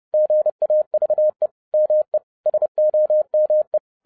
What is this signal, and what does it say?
Morse code, heard as a keyed tone, about 30 wpm, 610 Hz. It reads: GAVE ME SOME